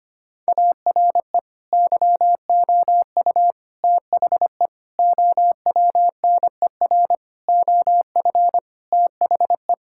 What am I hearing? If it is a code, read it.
Morse code, heard as a keyed tone, about 25 wpm, 700 Hz. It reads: ARE YOU THE OWNER OF THE
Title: Sentences from Top 1600 Words 25wpm